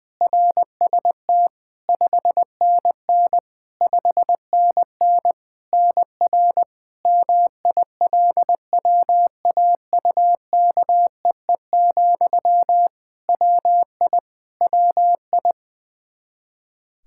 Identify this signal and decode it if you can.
Morse code, heard as a keyed tone, about 21 wpm, 700 Hz. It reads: RST 5NN 5NN NR MILWAUKEE, WI WI
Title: Practice QSO - Exchange 1 - Beginning - 20wpm